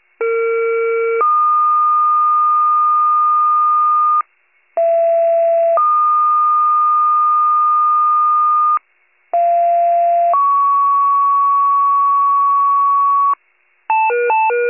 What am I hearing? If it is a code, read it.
Two-tone page, heard as a one-second tone, then a three-second tone, about 460 then 1220 Hz, about 670 then 1220 Hz, about 690 then 1120 Hz. It is heard from 0.2 to 13.3 s.